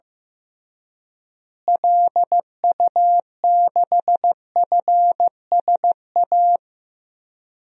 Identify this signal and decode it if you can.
Morse code, heard as a keyed tone, about 15 wpm, 700 Hz. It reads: LU6FSA
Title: DXCC-100 Call Signs - 6 Characters - Repeated 3x - 15wpm